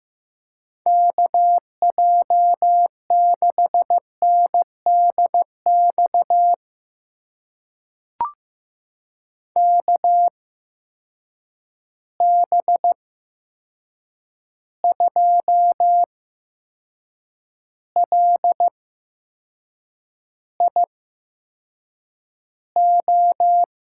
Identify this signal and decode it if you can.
Morse code, heard as a keyed tone, about 15 wpm, spaced as for 6 wpm, 700 Hz. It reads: KJ6NDX KB2LIO
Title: USA Call Signs - 6 Characters - 8x Character Spacing - 15wpm